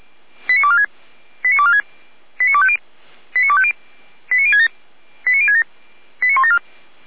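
Five-tone selective call (ZVEI), heard as quick runs of five tones, tones of about 80 ms. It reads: 89246 89246 89250 89250 89067 89067 89174
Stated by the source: ZVEI five-tone selective call received on an SDR, from the Feuerwehr Orlamünde audio server.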